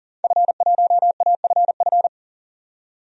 Morse code, heard as a keyed tone, about 40 wpm, 700 Hz. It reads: F1AFF
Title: DXCC-100 Call Signs - 5 Characters - 40wpm